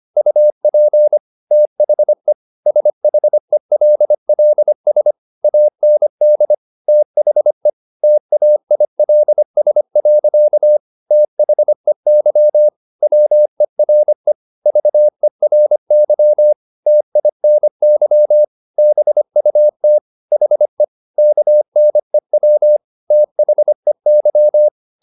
Morse code, heard as a keyed tone, about 25 wpm, 600 Hz. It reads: UP THE SHELLS AND THE TAILS. THEY WERE VERY TINY BUT HE KNEW THEY